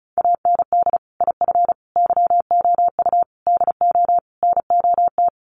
Morse code, heard as a keyed tone, about 34 wpm, 700 Hz. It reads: AND IF YOU DO NOT
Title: Sets of 5 Words 35wpm